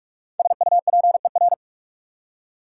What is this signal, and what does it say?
Morse code, heard as a keyed tone, about 44 wpm, 690 Hz. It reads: SUPER
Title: Prefixes and Suffixes 45wpm